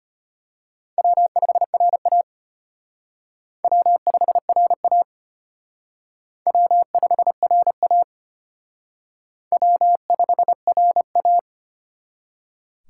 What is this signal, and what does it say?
Morse code, heard as a keyed tone, about 32 wpm, 700 Hz. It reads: W5RA W5RA W5RA W5RA